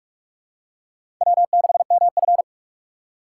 Morse code, heard as a keyed tone, about 44 wpm, 690 Hz. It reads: W6MF
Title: USA Call Signs - 4 Characters - 4x Character Spacing - 45wpm